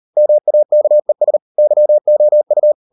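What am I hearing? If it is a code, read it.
Morse code, heard as a keyed tone, about 38 wpm, 600 Hz. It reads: MAKES YOU
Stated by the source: Morse code practice at 38 WPM, sending Ken Follett's novel Never in English, from CW med Gnister.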